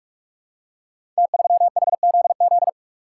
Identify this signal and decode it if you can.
Morse code, heard as a keyed tone, about 44 wpm, 700 Hz. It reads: T3HZZ
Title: Code Groups - 5 Letter:Numbers - 45wpm